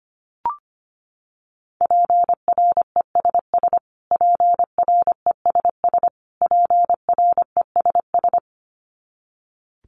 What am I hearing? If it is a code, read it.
Morse code, heard as a keyed tone, about 25 wpm, 700 Hz. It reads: PRESS PRESS PRESS
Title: Top 500 Words Repeated 25wpm